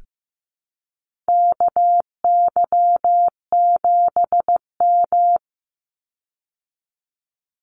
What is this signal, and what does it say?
Morse code, heard as a keyed tone, about 15 wpm, 700 Hz. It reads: KY7M
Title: Call Signs Repeated 3x - 15wpm